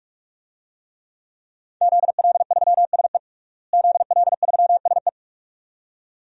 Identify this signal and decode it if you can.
Morse code, heard as a keyed tone, about 41 wpm, 700 Hz. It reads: ZL3SE ZL3SE
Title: DXCC-100 Call Signs - 5 Characters - Repeated 2x - 45wpm